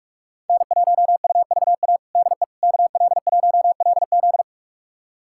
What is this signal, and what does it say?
Morse code, heard as a keyed tone, about 46 wpm, 700 Hz. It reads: N1UVA DE KL1LZ
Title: USA Callsign DE Callsign - 5 Characters - 45wpm